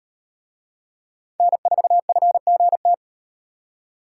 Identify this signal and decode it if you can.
Morse code, heard as a keyed tone, about 36 wpm, 700 Hz. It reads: N4FGT